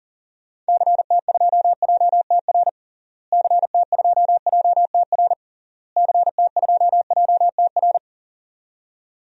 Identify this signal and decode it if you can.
Morse code, heard as a keyed tone, about 40 wpm, 700 Hz. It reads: CT2JTR CT2JTR CT2JTR
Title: DXCC-100 Call Signs - 6 Characters - Repeated 3x - 40wpm